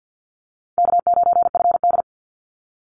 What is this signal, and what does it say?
Morse code, heard as a keyed tone, about 50 wpm, 700 Hz. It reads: K9FD
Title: Call Signs Repeated 3x - 50wpm